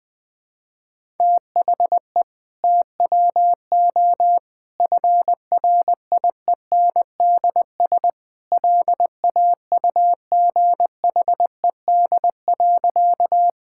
Morse code, heard as a keyed tone, about 20 wpm, 700 Hz. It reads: THE TWO FRIENDS LAUGHED.
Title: Sentences with 2-Word Phrases - Spoken Beginning - 20wpm